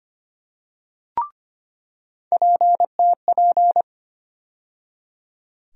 Morse code, heard as a keyed tone, about 25 wpm, 700 Hz. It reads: PTP
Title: USA Call Sign Postfixes - 3 Letters 25wpm